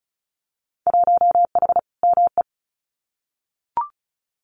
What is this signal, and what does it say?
Morse code, heard as a keyed tone, about 35 wpm, 700 Hz. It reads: 1H ME
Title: Winter Field Day Exchanges 35wpm